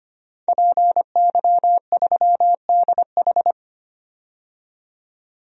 Morse code, heard as a keyed tone, about 25 wpm, 700 Hz. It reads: PY3DH
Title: DXCC-100 Call Signs - 5 Characters - 25wpm